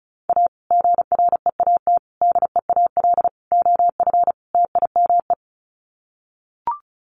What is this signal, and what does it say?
Morse code, heard as a keyed tone, about 34 wpm, 700 Hz. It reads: A GREAT DEAL OF TIME
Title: Sets of 5 Words Encore2 35wpm